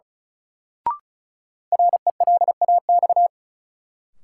Morse code, heard as a keyed tone, about 36 wpm, 700 Hz. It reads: RELAX
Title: Sentences from Top 1400 Words - New Word Drill Alt - 35wpm